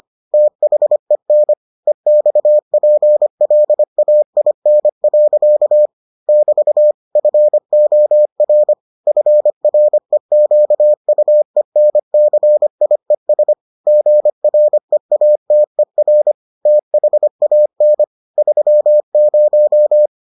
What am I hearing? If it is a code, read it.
Morse code, heard as a keyed tone, about 25 wpm, 600 Hz. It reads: THEN EXPLAIN. <BT> FOR FREQUENCIES GREATER THAN 30